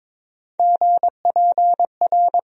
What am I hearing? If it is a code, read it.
Morse code, heard as a keyed tone, about 22 wpm, 700 Hz. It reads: GPR
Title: USA Call Sign Postfixes - 3 Letters 22wpm